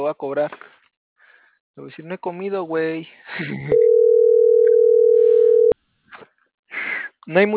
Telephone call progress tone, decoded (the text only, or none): ringback tone